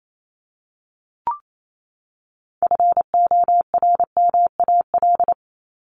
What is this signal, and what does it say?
Morse code, heard as a keyed tone, about 28 wpm, 700 Hz. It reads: FORMAL
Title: Top 1251-1500 Words 28wpm